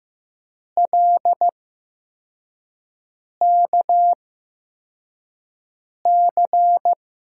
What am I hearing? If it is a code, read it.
Morse code, heard as a keyed tone, about 15 wpm, spaced as for 4 wpm, 700 Hz. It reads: LKC
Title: USA Call Signs - 6 Characters - 8x Character Spacing - 15wpm